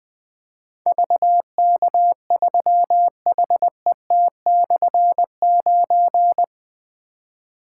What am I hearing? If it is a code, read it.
Morse code, heard as a keyed tone, about 20 wpm, 700 Hz. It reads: VK3HET/9
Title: Difficult Call Signs 20wpm